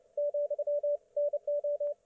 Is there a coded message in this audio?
Yes: Morse code from 0.2 to 1.9 s, heard as a keyed tone, about 28 wpm, 560 Hz, with radio noise. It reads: ,NO